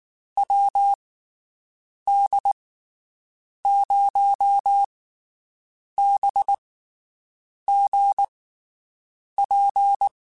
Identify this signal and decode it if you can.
Morse code, heard as a keyed tone, about 19 wpm, spaced as for 7 wpm, 800 Hz. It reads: WD0BGP
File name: (spaced) / QSOs Spaced-mp3s / WD0BGP 1 @18 sp